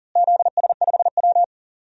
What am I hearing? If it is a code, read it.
Morse code, heard as a keyed tone, about 40 wpm, 700 Hz. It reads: ZS5W
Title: DXCC-100 Call Signs - 4 Characters - Repeated 2x - 40wpm